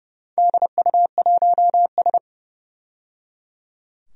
Morse code, heard as a keyed tone, about 30 wpm, 700 Hz. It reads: DU1S
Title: DXCC-100 Call Signs - 4 Characters - 30wpm